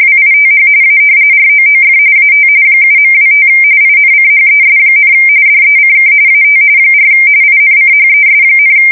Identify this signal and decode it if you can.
RTTY, about 45.45 baud. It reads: WITH A ROOF AT 7.25 INCHES, DECK AT 4 INCHES,
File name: fsk170 / battleshipnj